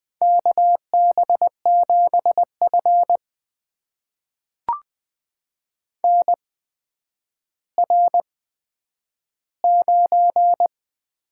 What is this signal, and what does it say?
Morse code, heard as a keyed tone, about 20 wpm, spaced as for 10 wpm, 700 Hz. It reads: KB7F NR9
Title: USA Call Signs - 4 Characters - 8x Character Spacing - 20wpm